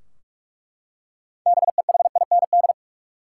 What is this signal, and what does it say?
Morse code, heard as a keyed tone, about 42 wpm, 700 Hz. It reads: BEHIND
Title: Six Letter Words - 45wpm